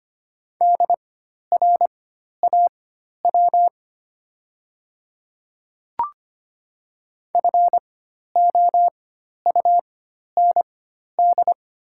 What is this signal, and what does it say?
Morse code, heard as a keyed tone, about 25 wpm, spaced as for 12 wpm, 690 Hz. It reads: DRAW FOUND
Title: 4x Character Spacing - Top 300 Words - 25wpm